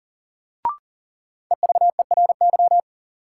Morse code, heard as a keyed tone, about 40 wpm, 700 Hz. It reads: EVERY
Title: Sentences from Top 200 Words - New Word Drill - 40wpm